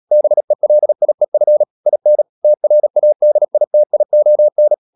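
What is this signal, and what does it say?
Morse code, heard as a keyed tone, about 37 wpm, 600 Hz. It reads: BELIEF IN TRADITION
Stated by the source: Morse code practice at 35–39 wpm sending English text of Ken Follett's novel Never, from CW med Gnister.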